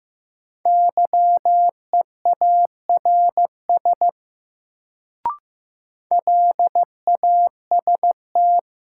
Morse code, heard as a keyed tone, about 15 wpm, 700 Hz. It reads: YEARS LAST
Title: Sets of 2 Words - New Word Drill - 15wpm